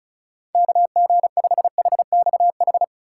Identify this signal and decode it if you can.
Morse code, heard as a keyed tone, about 36 wpm, 700 Hz. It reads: KG5HXH
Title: USA Call Signs - 6 Characters - 2x Character Spacing - 35wpm